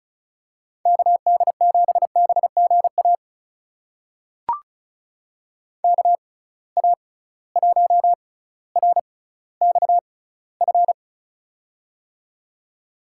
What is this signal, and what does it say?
Morse code, heard as a keyed tone, about 36 wpm, spaced as for 18 wpm, 700 Hz. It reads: KD7BGA KA1RXF
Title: USA Call Signs - 6 Characters - 6x Character Spacing - 35wpm